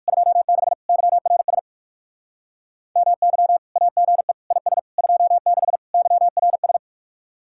Morse code, heard as a keyed tone, about 59 wpm, 700 Hz. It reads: 26 YRS MY AGE IS 26 YRS